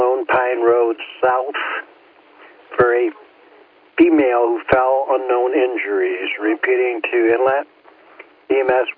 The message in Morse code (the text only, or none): none